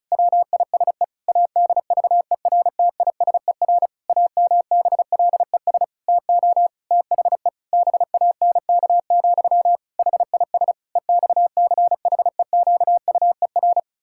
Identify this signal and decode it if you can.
Morse code, heard as a keyed tone, about 34 wpm, 700 Hz. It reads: WISE ADVERTISER AMBLES TO THE BANK, HIS EXCHEQUER
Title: Pangram Challenge v2 - 35wpm